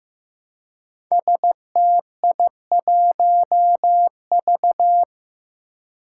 Morse code, heard as a keyed tone, about 15 wpm, 700 Hz. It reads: STI1V
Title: Five Challenging Letter-Number Combinations 15wpm